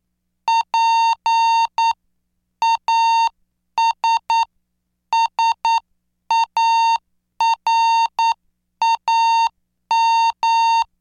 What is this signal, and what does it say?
Morse code, heard as a keyed tone, about 9 wpm, spaced as for 8 wpm, 910 Hz. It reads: PASSARAM